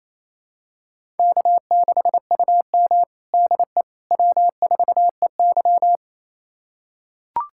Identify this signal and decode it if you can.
Morse code, heard as a keyed tone, about 28 wpm, 700 Hz. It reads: K6UM DE W4EY
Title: USA Callsign DE Callsign - 4 Characters - 28wpm